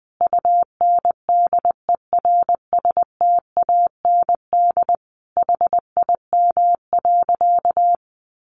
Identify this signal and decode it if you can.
Morse code, heard as a keyed tone, about 20 wpm, 700 Hz. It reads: UNDERSTAND HIM.
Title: Sentences from Top 600 Words 20wpm